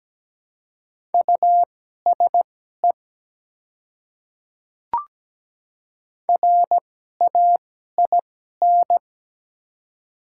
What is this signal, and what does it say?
Morse code, heard as a keyed tone, about 17 wpm, spaced as for 12 wpm, 700 Hz. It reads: USE RAIN
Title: 2x Character Spacing - Top 300 Words - 17wpm